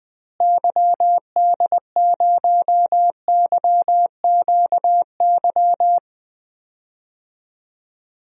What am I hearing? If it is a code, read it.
Morse code, heard as a keyed tone, about 21 wpm, 700 Hz. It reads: YD0YQY
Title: DXCC-100 Call Signs - 6 Characters - Repeated 2x - 20wpm